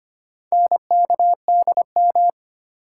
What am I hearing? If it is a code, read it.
Morse code, heard as a keyed tone, about 25 wpm, 700 Hz. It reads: NKDM
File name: practice / 25wpm / Four Letter-Number Combinations - 25wpm